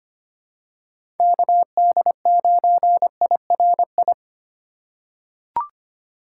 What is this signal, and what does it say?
Morse code, heard as a keyed tone, about 25 wpm, 700 Hz. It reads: KD9IRI
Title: USA Call Signs - 6 Characters - 3x Character Spacing - 25wpm